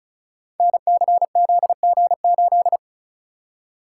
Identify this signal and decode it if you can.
Morse code, heard as a keyed tone, about 36 wpm, 700 Hz. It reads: NCZG8
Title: Code Groups - 5 Letter:Numbers - 35wpm